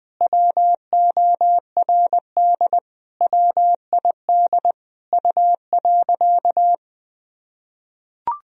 Morse code, heard as a keyed tone, about 20 wpm, 700 Hz. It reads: WORD WID U.
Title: Sentences from Top 300 Words and Abbreviations 20wpm